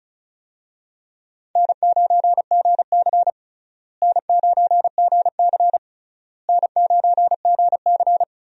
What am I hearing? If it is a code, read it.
Morse code, heard as a keyed tone, about 36 wpm, 700 Hz. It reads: N9GC N9GC N9GC
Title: USA Call Signs - 4 Characters - Repeated 3x - 35wpm